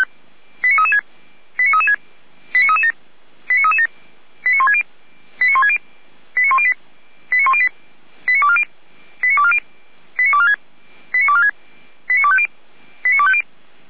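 Five-tone selective call (ZVEI), heard as quick runs of five tones, tones of about 70 ms. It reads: 89336 89336 89337 89337 89160 89160 89108 89108 89240 89240 89246 89246 89250 89250
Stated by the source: ZVEI five-tone selective call 89165 received by SDR.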